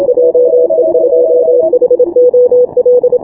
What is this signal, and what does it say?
Morse code, heard as a keyed tone, about 25 wpm, 500 Hz. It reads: JE2HOL
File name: final / W1UJ-168-140-MorseRunner